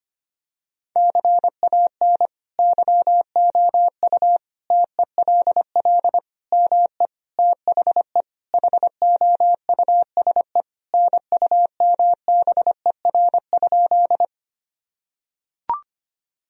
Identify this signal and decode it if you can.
Morse code, heard as a keyed tone, about 25 wpm, 700 Hz. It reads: CAN YOU TELL ME THE HOUSE NUMBER?
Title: Sentences from Top 1000 Words - Review Part 1 - 25wpm